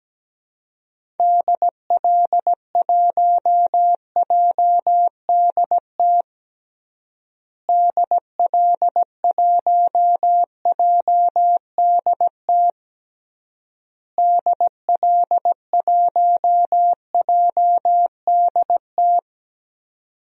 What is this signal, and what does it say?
Morse code, heard as a keyed tone, about 17 wpm, 700 Hz. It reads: DL1JDT DL1JDT DL1JDT
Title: DXCC-100 Call Signs - 6 Characters - Repeated 3x - 17wpm